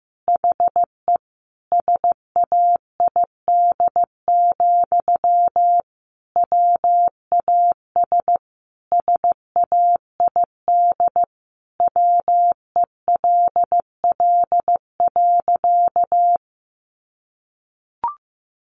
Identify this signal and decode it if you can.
Morse code, heard as a keyed tone, about 15 wpm, 700 Hz. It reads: HE SAID, WAS SAID WELL.
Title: Sentences from Top 1100 Words 15wpm